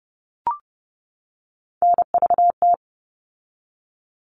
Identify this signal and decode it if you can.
Morse code, heard as a keyed tone, about 30 wpm, 700 Hz. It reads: NVT
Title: All Cut Numbers - 3-Digit 30wpm